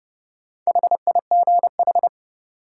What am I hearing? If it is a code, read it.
Morse code, heard as a keyed tone, about 30 wpm, 700 Hz. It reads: HIGH
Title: Sentences from Top 200 Words - New Word Drill - 30wpm